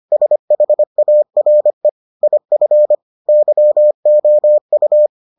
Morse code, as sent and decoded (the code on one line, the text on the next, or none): ... .... .- .-. / .. ..-. / -.-- --- ..-
SHARE IF YOU